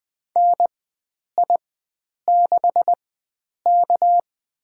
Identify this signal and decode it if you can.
Morse code, heard as a keyed tone, about 20 wpm, spaced as for 9 wpm, 700 Hz. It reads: NI6K
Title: USA Call Signs - 4 Characters - 4x Character Spacing - 20wpm